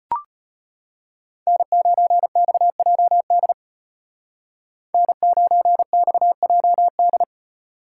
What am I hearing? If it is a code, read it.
Morse code, heard as a keyed tone, about 34 wpm, 700 Hz. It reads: N9XJD N9XJD